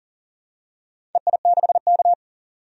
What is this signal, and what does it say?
Morse code, heard as a keyed tone, about 40 wpm, 700 Hz. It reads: EI6K
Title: DXCC-100 Call Signs - 4 Characters - 40wpm